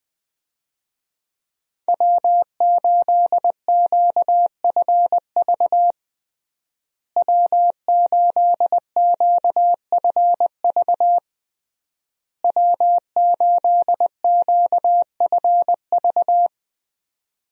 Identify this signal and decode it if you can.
Morse code, heard as a keyed tone, about 21 wpm, 700 Hz. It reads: W8QFV W8QFV W8QFV
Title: USA Call Signs - 5 Characters - Repeated 3x - 20wpm